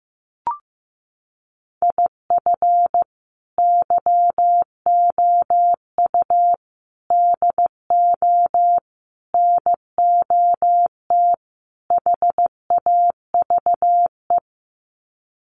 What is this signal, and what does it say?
Morse code, heard as a keyed tone, about 15 wpm, 700 Hz. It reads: IF YOU DO NOT HAVE
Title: Sets of 5 Words 15wpm